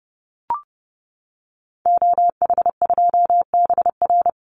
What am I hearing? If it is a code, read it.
Morse code, heard as a keyed tone, about 28 wpm, 700 Hz. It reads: OH2BR
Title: Call Signs - 30wpm